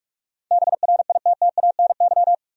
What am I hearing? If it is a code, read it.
Morse code, heard as a keyed tone, about 46 wpm, 700 Hz. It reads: BRITTANY
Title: USA Common Names - 45wpm